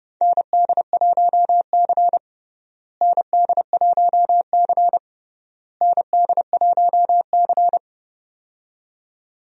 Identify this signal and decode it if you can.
Morse code, heard as a keyed tone, about 30 wpm, 700 Hz. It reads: ND1C ND1C ND1C
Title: USA Call Signs - 4 Characters - Repeated 3x - 30wpm